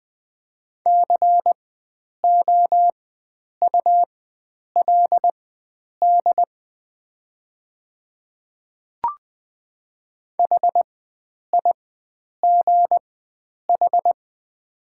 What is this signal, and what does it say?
Morse code, heard as a keyed tone, about 20 wpm, spaced as for 9 wpm, 700 Hz. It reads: COULD HIGH
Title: 4x Character Spacing - Top 300 Words - 20wpm